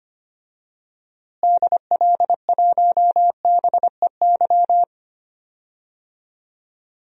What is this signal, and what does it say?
Morse code, heard as a keyed tone, about 25 wpm, 700 Hz. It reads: DL1BEY
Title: DXCC-100 Call Signs - 6 Characters - Repeated 2x - 25wpm